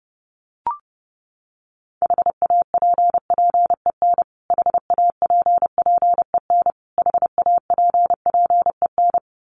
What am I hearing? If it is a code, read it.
Morse code, heard as a keyed tone, about 30 wpm, 700 Hz. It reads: HAPPEN HAPPEN HAPPEN
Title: Top 500 Words Repeated 30wpm